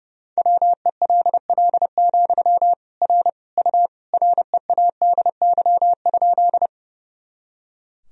Morse code, heard as a keyed tone, about 32 wpm, 700 Hz. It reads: WELL, R U READY?
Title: Sentences from Top 400 Words and Abbreviations 30wpm